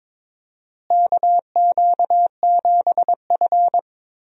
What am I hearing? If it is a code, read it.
Morse code, heard as a keyed tone, about 22 wpm, 700 Hz. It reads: KQ7F
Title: USA Call Signs - 4 Characters - Repeated 2x - 22wpm